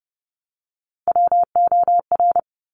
Morse code, heard as a keyed tone, about 30 wpm, 700 Hz. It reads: WOR
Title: Top Three Letter Combinations in Words 30wpm